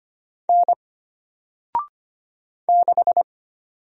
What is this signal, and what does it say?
Morse code, heard as a keyed tone, about 25 wpm, 700 Hz. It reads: N 6
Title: Single Letter-Number 25wpm